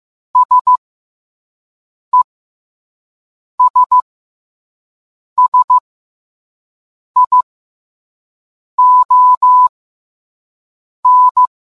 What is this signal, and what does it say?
Morse code, heard as a keyed tone, about 15 wpm, spaced as for 5 wpm, 1000 Hz. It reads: SESSION